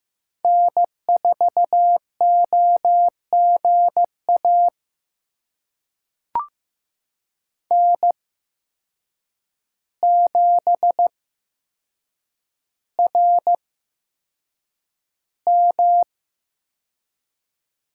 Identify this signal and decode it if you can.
Morse code, heard as a keyed tone, about 15 wpm, spaced as for 7 wpm, 700 Hz. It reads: N4OGA N7RM
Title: USA Call Signs - 5 Characters - 8x Character Spacing - 15wpm